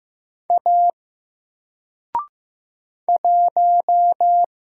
Morse code, heard as a keyed tone, about 15 wpm, 700 Hz. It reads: A 1
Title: Single Letter-Number 15wpm